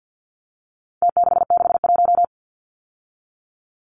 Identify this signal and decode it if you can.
Morse code, heard as a keyed tone, about 50 wpm, 700 Hz. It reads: T661
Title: Numbers - Contest Style - 4-Digit 50wpm